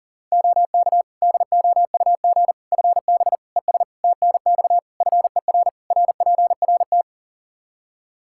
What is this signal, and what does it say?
Morse code, heard as a keyed tone, about 40 wpm, 700 Hz. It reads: OK DOUG FB ES TNX FER RPRT
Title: Practice QSO - Exchange 2 - Info - 40wpm